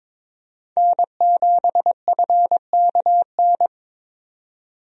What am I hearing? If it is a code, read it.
Morse code, heard as a keyed tone, about 22 wpm, 700 Hz. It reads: N7FKN
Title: USA Call Signs - 5 Characters - 22wpm